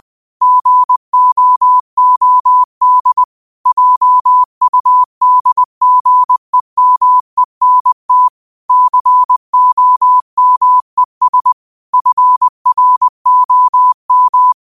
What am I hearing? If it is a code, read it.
Morse code, heard as a keyed tone, about 20 wpm, 1000 Hz. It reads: GOOD JUDGEMENT COMES FROM